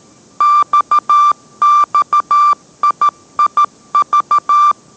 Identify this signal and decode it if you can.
Morse code, heard as a keyed tone, about 13 wpm, 1180 Hz. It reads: XXIIV